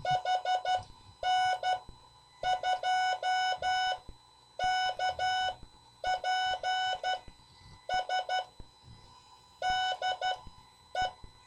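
Morse code, heard as a keyed tone, about 12 wpm, spaced as for 8 wpm, 740 Hz. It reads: H N2KPS DE
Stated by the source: CW class lesson from The Martian Net with instructor N2KPS.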